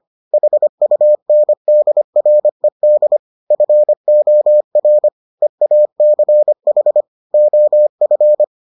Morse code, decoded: HUNDRED FOR EACH OF

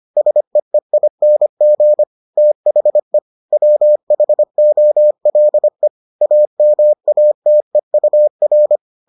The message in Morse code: SEEING THE WHOLE AMATEUR

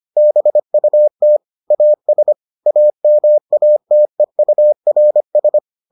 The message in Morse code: BUT AS AMATEURS

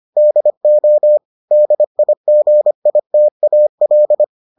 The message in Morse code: DO DIGITAL

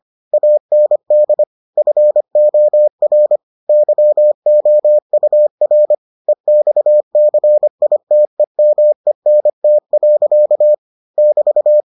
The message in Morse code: AND FOR YOUR EXCITEMENT. <BT>

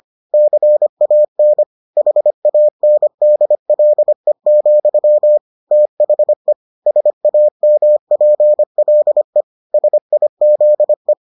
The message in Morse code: CAN HANDLE, THE SAMPLE SIZE

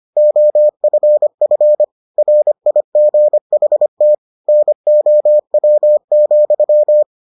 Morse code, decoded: OFF RIGHT NOW,

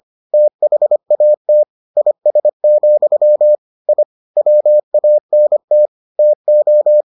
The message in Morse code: THAT IS, I WANT TO

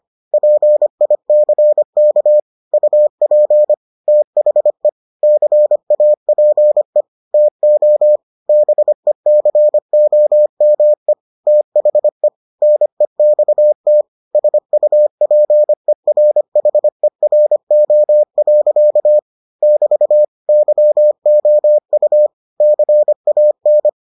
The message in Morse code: PICK UP THE CAPE TO BECOME THE NEXT SUPERHERO. <BT> YOU CAN

